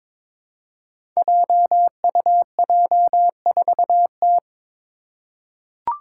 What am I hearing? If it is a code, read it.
Morse code, heard as a keyed tone, about 22 wpm, 700 Hz. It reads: JUJ4T
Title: Five Challenging Letter-Number Combinations 22wpm